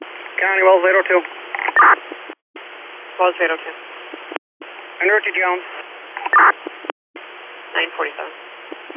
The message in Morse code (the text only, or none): none